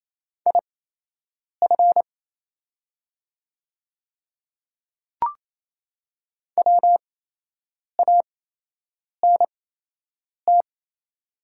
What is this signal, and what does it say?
Morse code, heard as a keyed tone, about 28 wpm, spaced as for 8 wpm, 700 Hz. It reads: IF WANT
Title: 8x Character Spacing - Top 300 Words - 28wpm